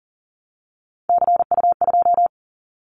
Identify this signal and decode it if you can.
Morse code, heard as a keyed tone, about 37 wpm, 690 Hz. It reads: CU2
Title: Callsign Prefixes for Top 100 DXCC - 40wpm